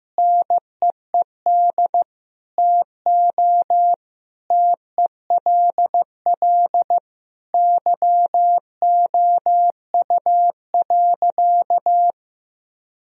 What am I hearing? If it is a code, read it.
Morse code, heard as a keyed tone, about 15 wpm, 700 Hz. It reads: NEED TO TELL YOU.
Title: Sentences from Top 1000 Words - Review Part 1 - 15wpm